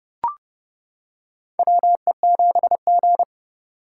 Morse code, heard as a keyed tone, about 32 wpm, 700 Hz. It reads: WE7G
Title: K1USN SST QSO Practice - 30wpm